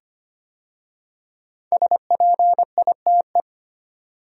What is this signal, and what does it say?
Morse code, heard as a keyed tone, about 25 wpm, 700 Hz. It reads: SPITE
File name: 5 Letter Words - First 3 Letters Given - 25wpm